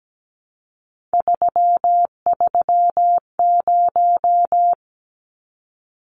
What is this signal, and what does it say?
Morse code, heard as a keyed tone, about 17 wpm, 700 Hz. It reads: 330
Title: Numbers - 3-Digit 17wpm